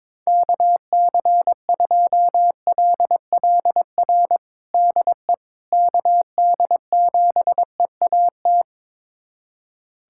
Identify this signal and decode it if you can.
Morse code, heard as a keyed tone, about 22 wpm, 700 Hz. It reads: KC2LLR DE KD7EAT
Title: USA Callsign DE Callsign - 6 Characters - 22wpm